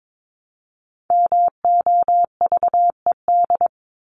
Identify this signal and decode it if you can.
Morse code, heard as a keyed tone, about 22 wpm, 700 Hz. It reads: MOVED
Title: ED Suffix - Top 100 Words with Base Words 22wpm